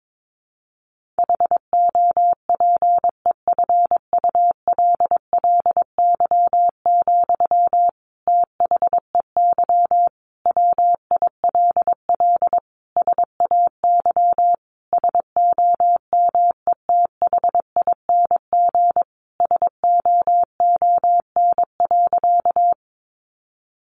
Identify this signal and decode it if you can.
Morse code, heard as a keyed tone, about 22 wpm, 700 Hz. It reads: HOPEFULLY, THEY WILL SAY SOMETHING SOON.
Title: Sentences from Top 1500 Words 22wpm